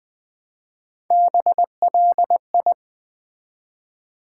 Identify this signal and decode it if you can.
Morse code, heard as a keyed tone, about 20 wpm, 700 Hz. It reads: BLI
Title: Build A Word - 1 Letter At A Time - 5-Letter Words - 20wpm